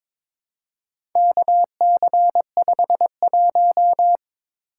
Morse code, heard as a keyed tone, about 22 wpm, 700 Hz. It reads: KC51
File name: practice / 22wpm / Four Letter-Number Combinations - 22wpm